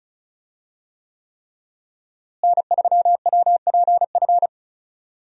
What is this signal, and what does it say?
Morse code, heard as a keyed tone, about 34 wpm, 700 Hz. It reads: N3WPF
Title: USA Call Signs - 5 Characters - Repeated 3x - 35wpm